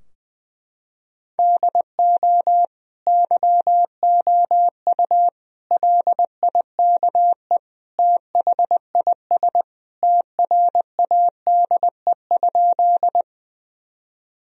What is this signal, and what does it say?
Morse code, heard as a keyed tone, about 20 wpm, 700 Hz. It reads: DO YOU LIKE THIS TRADE?
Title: Sentences from Top 1000 Words - Review Part 4 - 20wpm